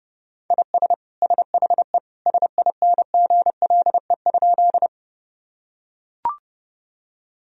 Morse code, heard as a keyed tone, about 30 wpm, 700 Hz. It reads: IS SHE SINGLE?
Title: Sentences from Top 1000 Words - Review Part 4 - 30wpm